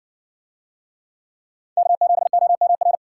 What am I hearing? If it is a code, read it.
Morse code, heard as a keyed tone, about 60 wpm, 700 Hz. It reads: K7OMR